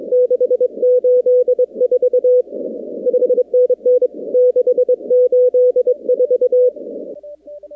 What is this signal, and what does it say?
Morse code, heard as a keyed tone, about 24 wpm, 500 Hz. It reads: 684 5NN 684
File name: contests / 2019-cw-wpx-kt5j-k5pi- -2330 0-45